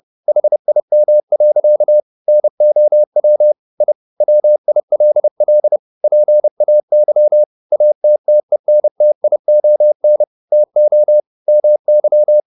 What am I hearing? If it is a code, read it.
Morse code, heard as a keyed tone, about 30 wpm, 600 Hz. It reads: HIM. NOW I WILL PAY ATTENTION TO MY
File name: the old man and the sea / the old man and the sea 1 30wpm